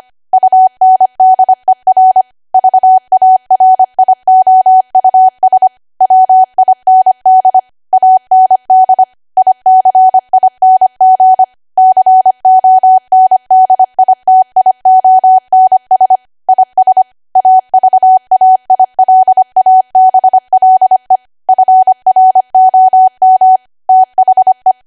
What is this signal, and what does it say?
Morse code, heard as a keyed tone, about 25 wpm, 750 Hz. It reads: UNDER VARIOUS WIND AND ICING CONDITIONS IS AVAILABLE FROM THE
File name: Archive / 25 WPM / 25WPM